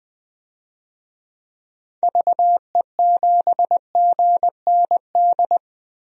Morse code, heard as a keyed tone, about 20 wpm, 700 Hz. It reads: VE7GND
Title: DXCC-100 Call Signs - 6 Characters - 20wpm